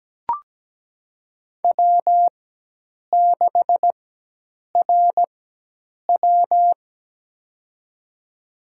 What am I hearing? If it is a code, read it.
Morse code, heard as a keyed tone, about 17 wpm, spaced as for 8 wpm, 700 Hz. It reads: W6RW
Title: USA Call Signs - 4 Characters - 4x Character Spacing - 17wpm